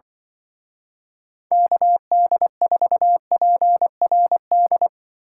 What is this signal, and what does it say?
Morse code, heard as a keyed tone, about 25 wpm, 700 Hz. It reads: KD4PRD
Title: Speed Racing - USA 2x3 Call Signs 20wpm